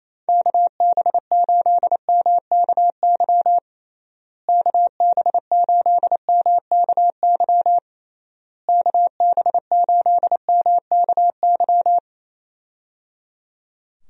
Morse code, heard as a keyed tone, about 28 wpm, 700 Hz. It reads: KB8MKY KB8MKY KB8MKY
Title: USA Call Signs - 6 Characters - Repeated 3x - 28wpm